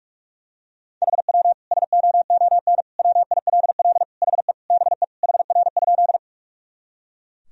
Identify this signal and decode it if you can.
Morse code, heard as a keyed tone, about 41 wpm, 700 Hz. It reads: HW SOON WILL HE BE HR?
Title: Sentences from Top 400 Words and Abbreviations 45wpm